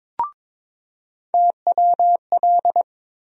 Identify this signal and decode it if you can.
Morse code, heard as a keyed tone, about 22 wpm, 700 Hz. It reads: TWL
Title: USA Call Sign Postfixes - 3 Letters 22wpm